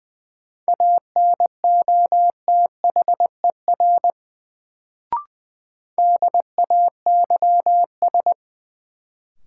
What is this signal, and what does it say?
Morse code, heard as a keyed tone, about 20 wpm, 700 Hz. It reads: ANOTHER DAYS
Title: Sets of 5 Words Encore - New Word Drill - 20wpm